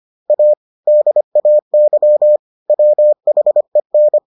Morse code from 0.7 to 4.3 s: DAY WHEN

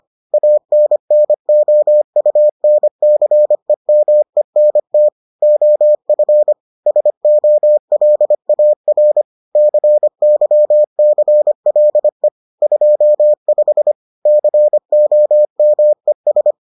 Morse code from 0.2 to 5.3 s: ANNOUNCEMENT

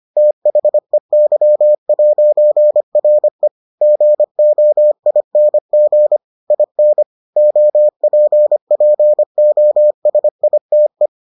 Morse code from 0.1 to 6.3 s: THEY'RE GOING

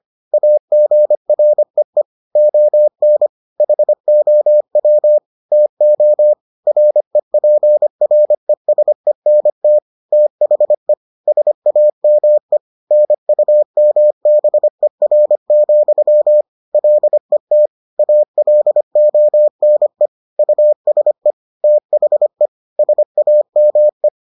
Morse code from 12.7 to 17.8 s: NUMBER, LET

